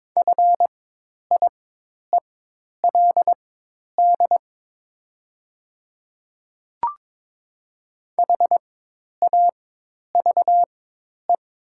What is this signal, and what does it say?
Morse code, heard as a keyed tone, about 22 wpm, spaced as for 10 wpm, 700 Hz. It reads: FIELD HAVE
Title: 4x Character Spacing - Top 500 Words - 22wpm